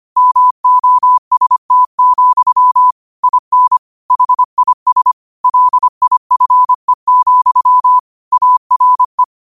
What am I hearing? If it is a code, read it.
Morse code, heard as a keyed tone, about 25 wpm, 1000 Hz. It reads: MOST, IN HIS LIFE, ARE
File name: qotdfiles / 25wpm